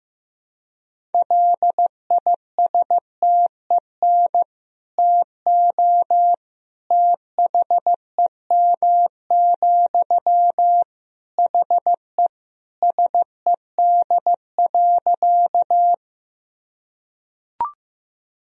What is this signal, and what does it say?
Morse code, heard as a keyed tone, about 15 wpm, 700 Hz. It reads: LISTEN TO THEM, HE SED.
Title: Sentences from Top 400 Words and Abbreviations 15wpm